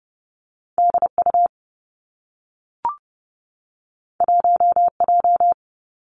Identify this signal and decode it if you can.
Morse code, heard as a keyed tone, about 30 wpm, 700 Hz. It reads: D U 1 J
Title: Two Challenging Letter-Number Combinations 30wpm